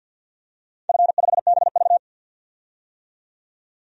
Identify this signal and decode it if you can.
Morse code, heard as a keyed tone, about 51 wpm, 700 Hz. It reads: F5BV
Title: DXCC-100 Call Signs - 4 Characters - 50wpm